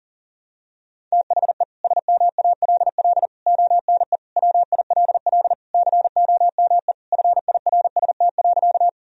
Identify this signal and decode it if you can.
Morse code, heard as a keyed tone, about 37 wpm, 700 Hz. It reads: THE SMALL ONE WILL COME FIRST.
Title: Sentences from Top 1000 Words - Review Part 1 - 40wpm